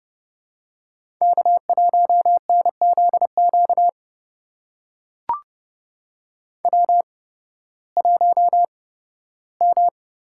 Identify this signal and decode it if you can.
Morse code, heard as a keyed tone, about 30 wpm, spaced as for 16 wpm, 700 Hz. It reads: K1NZQ W1M
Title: USA Call Signs - 5 Characters - 8x Character Spacing - 30wpm